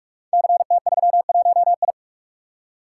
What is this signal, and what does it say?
Morse code, heard as a keyed tone, about 44 wpm, 690 Hz. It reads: CT31I
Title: Code Groups - 5 Letter:Numbers - 45wpm